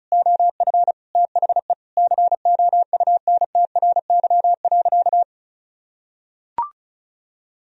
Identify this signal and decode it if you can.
Morse code, heard as a keyed tone, about 36 wpm, 700 Hz. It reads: OF THE COUNTRY.
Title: Sentences from Top 300 Words and Abbreviations 35wpm